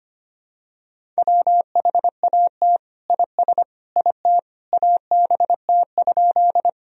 Morse code, heard as a keyed tone, about 25 wpm, 700 Hz. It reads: WHAT IS IT ABT?
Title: Sentences from Top 100 Words and Abbreviations 25wpm